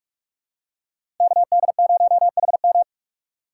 Morse code, heard as a keyed tone, about 44 wpm, 690 Hz. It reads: KD0HM
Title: USA Call Signs - 5 Characters - 2x Character Spacing - 45wpm